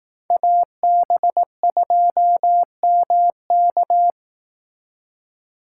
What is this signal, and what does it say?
Morse code, heard as a keyed tone, about 18 wpm, 700 Hz. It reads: AB2MK
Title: Speed Racing - USA 2x2 Call Signs 15wpm